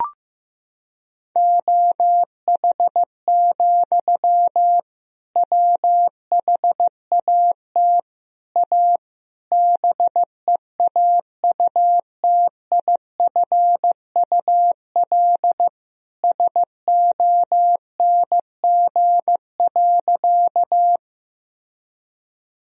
Morse code, heard as a keyed tone, about 15 wpm, 700 Hz. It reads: OH, WHAT A BEAUTIFUL SONG.
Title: Sentences from Top 1000 Words - Review Part 2 - 15wpm